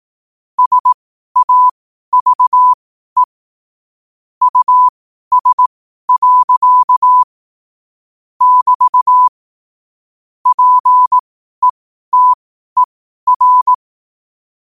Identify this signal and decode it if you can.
Morse code, heard as a keyed tone, about 18 wpm, spaced as for 12 wpm, 1000 Hz. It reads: SAVE US. <BT> PETER
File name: qotdfiles / 12wpm